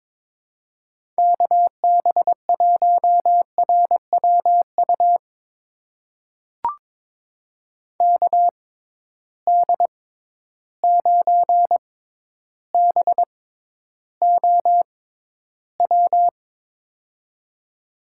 Morse code, heard as a keyed tone, about 21 wpm, spaced as for 11 wpm, 700 Hz. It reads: KB1RWU KD9BOW
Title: USA Call Signs - 6 Characters - 6x Character Spacing - 22wpm